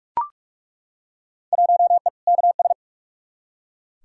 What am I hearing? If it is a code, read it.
Morse code, heard as a keyed tone, about 43 wpm, 700 Hz. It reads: 1E KS
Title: Field Day Exchanges - 45wpm